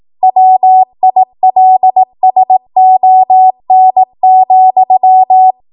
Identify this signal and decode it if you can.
Morse code, heard as a keyed tone, about 18 wpm, 750 Hz. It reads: WILSON,